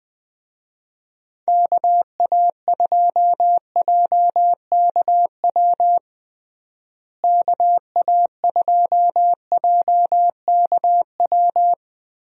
Morse code, heard as a keyed tone, about 20 wpm, 700 Hz. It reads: KA2JKW KA2JKW
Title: USA Call Signs - 6 Characters - Repeated 3x - 20wpm